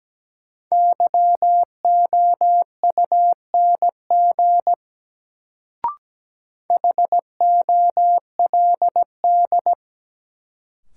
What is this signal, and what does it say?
Morse code, heard as a keyed tone, about 17 wpm, 700 Hz. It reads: YOUNG HOLD
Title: Sentences from Top 400 Words - New Word Drill - 17wpm